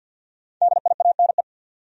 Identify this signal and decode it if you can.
Morse code, heard as a keyed tone, about 51 wpm, 700 Hz. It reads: DIANE
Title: USA Common Names - Max 5 Letters - 50wpm